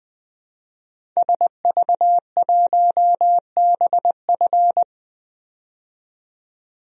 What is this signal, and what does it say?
Morse code, heard as a keyed tone, about 20 wpm, 700 Hz. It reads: SV1BF
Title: DXCC-100 Call Signs - 5 Characters - 20wpm